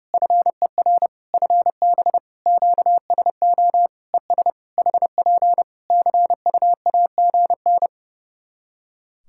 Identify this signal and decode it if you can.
Morse code, heard as a keyed tone, about 30 wpm, 700 Hz. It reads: FER FB QSO ES HP CUAGN
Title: K1USN SST QSO Practice 30wpm